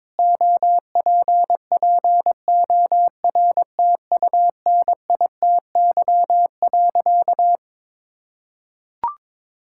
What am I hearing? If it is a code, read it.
Morse code, heard as a keyed tone, about 22 wpm, 700 Hz. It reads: OPPORTUNITY.
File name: practice / 22wpm / Sentences with 4-Word Phrases - 22wpm